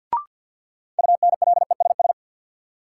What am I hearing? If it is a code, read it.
Morse code, heard as a keyed tone, about 52 wpm, 700 Hz. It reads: UNLESS
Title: Sentences from Top 1300 Words - New Word Drill Alt - 50wpm